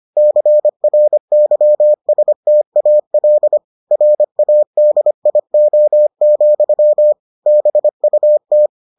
Morse code, heard as a keyed tone, about 25 wpm, 600 Hz. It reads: CRYSTAL RADIO, BUT